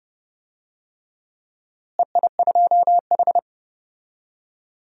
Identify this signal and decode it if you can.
Morse code, heard as a keyed tone, about 30 wpm, 700 Hz. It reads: EI2H
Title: DXCC-100 Call Signs - 4 Characters - Repeated 3x - 30wpm